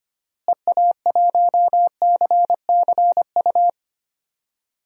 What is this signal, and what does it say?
Morse code, heard as a keyed tone, about 25 wpm, 700 Hz. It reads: EA1CCU
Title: DXCC-100 Call Signs - 6 Characters - Repeated 3x - 25wpm